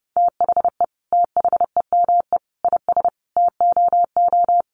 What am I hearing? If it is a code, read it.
Morse code, heard as a keyed tone, about 30 wpm, 700 Hz. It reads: THE THEME IS TOO
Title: Sentences from Top 1300 Words 30wpm